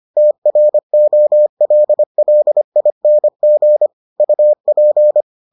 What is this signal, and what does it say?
Morse code, heard as a keyed tone, about 25 wpm, 600 Hz. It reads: TROLLING UP